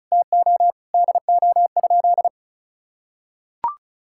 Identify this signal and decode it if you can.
Morse code, heard as a keyed tone, about 36 wpm, 700 Hz. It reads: TO DO?
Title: Sentences from Top 1600 Words 35wpm